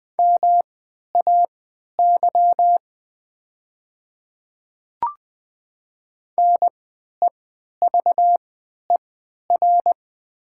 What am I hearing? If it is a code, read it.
Morse code, heard as a keyed tone, about 20 wpm, spaced as for 11 wpm, 700 Hz. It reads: MAY NEVER
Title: 3x Character Spacing - Top 300 Words - 20wpm